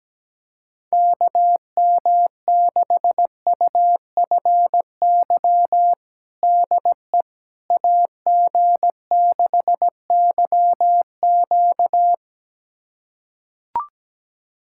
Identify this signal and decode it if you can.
Morse code, heard as a keyed tone, about 17 wpm, 700 Hz. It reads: KM6UFY DE AG6YQ
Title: USA Callsign DE Callsign 17wpm